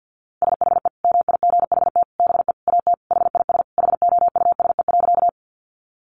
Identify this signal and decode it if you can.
Morse code, heard as a keyed tone, about 50 wpm, 700 Hz. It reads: SHE MIGHT BE AT HIS HOUSE.
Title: Sentences from Top 300 Words 50wpm